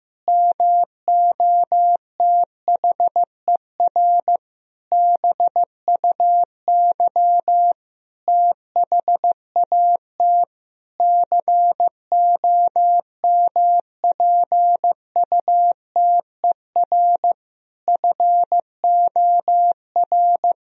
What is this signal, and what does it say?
Morse code, heard as a keyed tone, about 15 wpm, 700 Hz. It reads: MOTHER BUY THAT COMPUTER FOR
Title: Sentences from Top 1000 Words - Review Part 3 - 15wpm